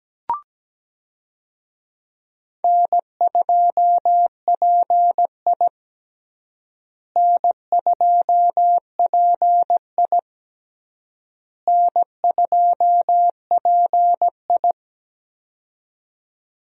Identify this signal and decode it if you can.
Morse code, heard as a keyed tone, about 17 wpm, 700 Hz. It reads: N2PI N2PI N2PI
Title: USA Call Signs - 4 Characters - Repeated 3x - 17wpm